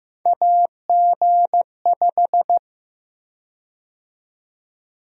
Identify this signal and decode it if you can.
Morse code, heard as a keyed tone, about 15 wpm, 700 Hz. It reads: AG5
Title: USA Call Sign Prefixes - 15wpm